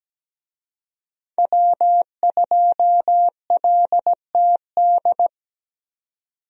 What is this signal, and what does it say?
Morse code, heard as a keyed tone, about 17 wpm, 700 Hz. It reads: W2LTD
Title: USA Call Signs - 5 Characters - 17wpm